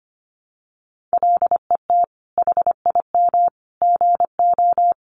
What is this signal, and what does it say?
Morse code, heard as a keyed tone, about 25 wpm, 700 Hz. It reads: LET HIM GO
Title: Sentences from Top 500 Words 25wpm